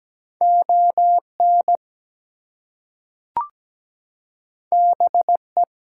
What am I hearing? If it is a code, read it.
Morse code, heard as a keyed tone, about 17 wpm, 700 Hz. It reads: ON BE
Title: Two Letter Words - 17wpm